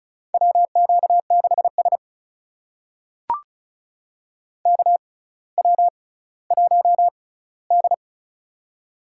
Morse code, heard as a keyed tone, about 36 wpm, spaced as for 18 wpm, 700 Hz. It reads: WQ6S KW1D
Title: USA Call Signs - 4 Characters - 6x Character Spacing - 35wpm